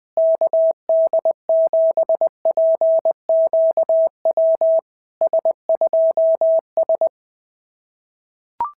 Morse code, heard as a keyed tone, about 20 wpm, 660 Hz. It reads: KD7PQW S2S
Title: SOTA QSO Practice 20wpm